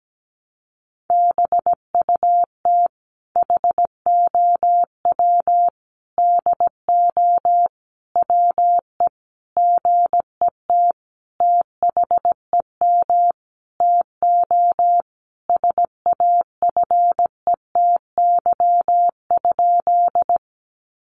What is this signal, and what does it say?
Morse code, heard as a keyed tone, about 17 wpm, 700 Hz. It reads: BUT HOW DO WE GET THEM TO SAFETY?
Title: Sentences from Top 700 Words 17wpm